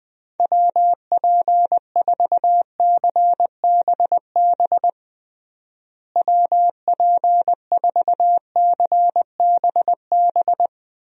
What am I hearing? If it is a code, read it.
Morse code, heard as a keyed tone, about 20 wpm, 700 Hz. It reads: WP4CBB WP4CBB
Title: USA Call Signs - 6 Characters - Repeated 3x - 20wpm